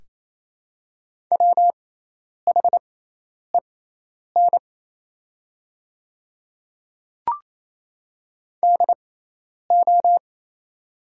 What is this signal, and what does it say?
Morse code, heard as a keyed tone, about 28 wpm, spaced as for 10 wpm, 690 Hz. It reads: WHEN DO
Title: 6x Character Spacing - Top 500 Words - 28wpm